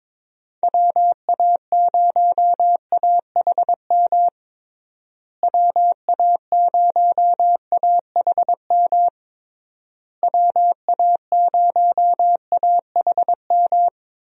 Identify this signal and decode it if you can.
Morse code, heard as a keyed tone, about 22 wpm, 700 Hz. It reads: WA0AHM WA0AHM WA0AHM
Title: USA Call Signs - 6 Characters - Repeated 3x - 22wpm